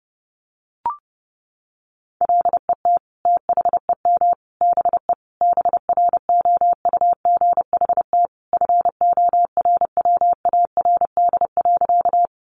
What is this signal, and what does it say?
Morse code, heard as a keyed tone, about 30 wpm, 700 Hz. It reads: LET THEM BE BROUGHT FORWARD.
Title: Sentences from Top 1300 Words 30wpm